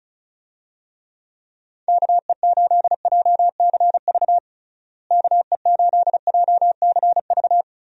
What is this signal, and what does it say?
Morse code, heard as a keyed tone, about 36 wpm, 700 Hz. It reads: KE8JCV KE8JCV
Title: USA Call Signs - 6 Characters - Repeated 2x - 35wpm